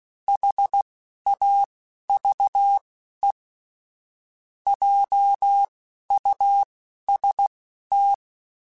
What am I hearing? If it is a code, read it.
Morse code, heard as a keyed tone, about 16 wpm, spaced as for 11 wpm, 800 Hz. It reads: HAVE JUST
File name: Technical Subjects / Technical Subjects-mp3s / OR Gate @10